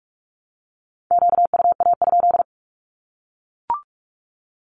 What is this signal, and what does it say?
Morse code, heard as a keyed tone, about 44 wpm, 700 Hz. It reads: QUA?
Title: Q-Codes Repeated 3x 45wpm